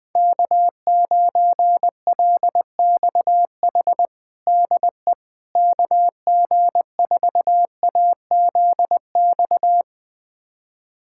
Morse code, heard as a keyed tone, about 20 wpm, 700 Hz. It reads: K9LXH DE KG4AZX
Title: USA Callsign DE Callsign 20wpm